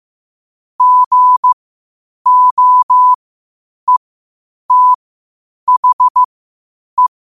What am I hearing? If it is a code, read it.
Morse code, heard as a keyed tone, about 15 wpm, spaced as for 8 wpm, 1000 Hz. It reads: GOETHE
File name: qotdfiles / 8wpm